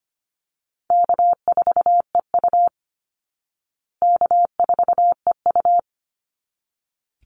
Morse code, heard as a keyed tone, about 25 wpm, 700 Hz. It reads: K4EU K4EU
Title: Call Signs Repeated 3x - 25wpm